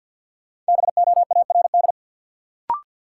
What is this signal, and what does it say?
Morse code, heard as a keyed tone, about 52 wpm, 700 Hz. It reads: BOARD
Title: Sentences from Top 700 Words - New Word Drill - 50wpm